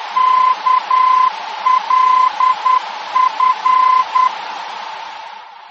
Morse code, heard as a keyed tone, about 10 wpm, 1020 Hz. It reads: KLF
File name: germany / mx / KLF-D-VOR